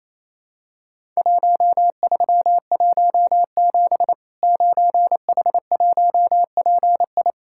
Morse code, heard as a keyed tone, about 28 wpm, 700 Hz. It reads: 1317 9H1PI
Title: Worked All Europe DX Contest - QTC Exchange - 28wpm